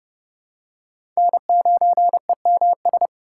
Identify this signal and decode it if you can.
Morse code, heard as a keyed tone, about 32 wpm, 700 Hz. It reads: N9EMS
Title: USA Call Signs - 5 Characters - 2x Character Spacing - 30wpm